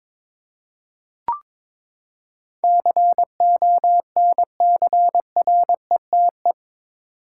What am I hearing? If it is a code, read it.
Morse code, heard as a keyed tone, about 22 wpm, 700 Hz. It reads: CONCRETE
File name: practice / 22wpm / Eight Letter Words - 22wpm